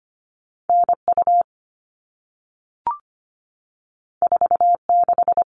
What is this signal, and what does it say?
Morse code, heard as a keyed tone, about 25 wpm, 700 Hz. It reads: N U 4 6
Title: All Cut Numbers - 2-Digit 25wpm